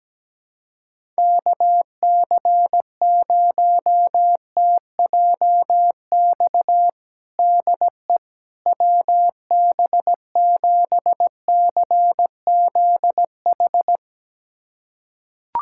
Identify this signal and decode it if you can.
Morse code, heard as a keyed tone, about 17 wpm, 700 Hz. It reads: KC0TJX DE WB7CZH
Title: USA Callsign DE Callsign 17wpm